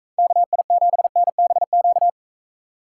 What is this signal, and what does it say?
Morse code, heard as a keyed tone, about 43 wpm, 700 Hz. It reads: KI7NBQ